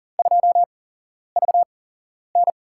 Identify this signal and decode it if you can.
Morse code, heard as a keyed tone, about 40 wpm, spaced as for 11 wpm, 700 Hz. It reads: 2VN
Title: USA Call Signs - 8x Character Spacing - 40wpm